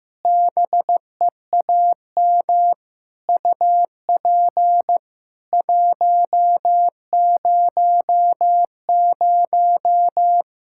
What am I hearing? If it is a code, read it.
Morse code, heard as a keyed tone, about 15 wpm, 700 Hz. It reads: BEAM UP 100
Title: Practice QSO - Exchange 2 - Info - 15wpm